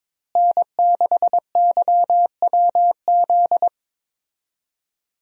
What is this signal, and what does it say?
Morse code, heard as a keyed tone, about 22 wpm, 700 Hz. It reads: N6YWZ
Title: USA Call Signs - 5 Characters - Repeated 2x - 22wpm